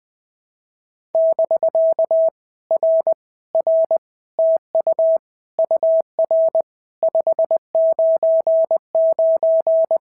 Morse code, heard as a keyed tone, about 20 wpm, 650 Hz. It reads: <BK> R R TU UR 599
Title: POTA QSO Practice - 20wpm